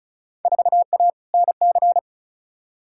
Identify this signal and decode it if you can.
Morse code, heard as a keyed tone, about 34 wpm, 700 Hz. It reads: 4A NC
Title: Field Day Exchanges - 35wpm